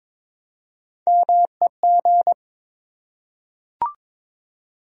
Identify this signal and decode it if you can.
Morse code, heard as a keyed tone, about 21 wpm, 700 Hz. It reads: MEG
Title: USA Common Names - Max 3 Letters - 22wpm